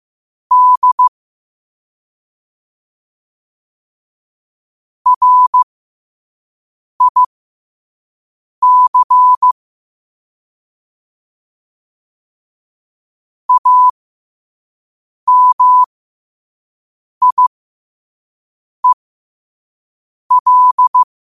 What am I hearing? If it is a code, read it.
Morse code, heard as a keyed tone, about 15 wpm, spaced as for 5 wpm, 1000 Hz. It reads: D RIC AMIEL